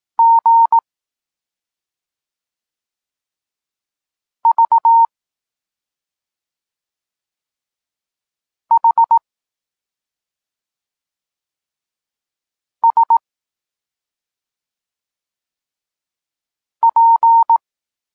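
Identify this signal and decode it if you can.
Morse code, heard as a keyed tone, about 18 wpm, 920 Hz. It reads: G V H S P
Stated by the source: Polish Morse lesson teaching the letter C, from SP9MOA Niepołomicki Klub Krótkofalowców.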